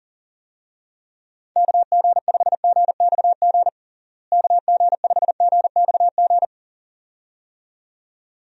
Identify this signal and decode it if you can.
Morse code, heard as a keyed tone, about 40 wpm, 700 Hz. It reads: KG5GXG KG5GXG
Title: USA Call Signs - 6 Characters - Repeated 2x - 40wpm